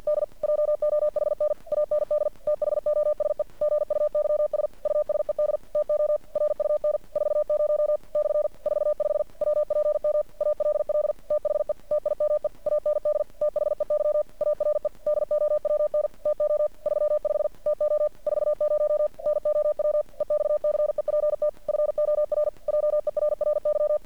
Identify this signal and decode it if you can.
Morse code, heard as a keyed tone, about 50 wpm, 600 Hz. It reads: D JOHN AND THOSE GUYS USED TO RUN 40 / 45 WPM ALL THE TIME AND THEY ARE DOWN TO 35 TO 40 NOW EXCEPT FOR JERRY
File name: archived / CRSnet